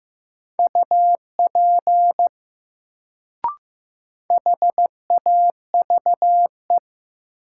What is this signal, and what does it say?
Morse code, heard as a keyed tone, about 15 wpm, 700 Hz. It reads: UP HAVE
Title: Sentences from Top 100 Words - New Word Drill - 15wpm